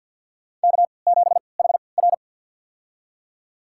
Morse code, heard as a keyed tone, about 51 wpm, spaced as for 28 wpm, 700 Hz. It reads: K7HR
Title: USA Call Signs - 4 Characters - 3x Character Spacing - 50wpm